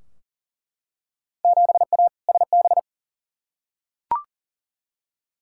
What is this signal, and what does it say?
Morse code, heard as a keyed tone, about 40 wpm, 700 Hz. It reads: 7A SB
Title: Field Day Exchanges Encore - 40wpm